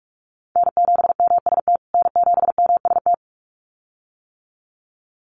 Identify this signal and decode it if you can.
Morse code, heard as a keyed tone, about 44 wpm, 690 Hz. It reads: N7MST N7MST
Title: USA Call Signs Repeated 2x 45wpm